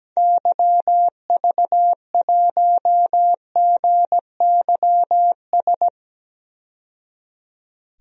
Morse code, heard as a keyed tone, about 17 wpm, 700 Hz. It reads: YV1GYS
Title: DXCC-100 Call Signs - 6 Characters - 17wpm